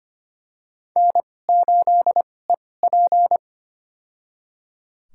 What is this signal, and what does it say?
Morse code, heard as a keyed tone, about 25 wpm, spaced as for 18 wpm, 700 Hz. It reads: N8EP
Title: USA Call Signs - 2x Character Spacing - 25wpm